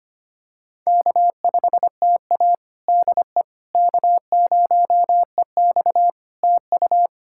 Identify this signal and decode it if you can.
Morse code, heard as a keyed tone, about 25 wpm, 700 Hz. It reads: K5TA DE K0EX TU
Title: Practice QSO - Exchange 4 - Ending - 25wpm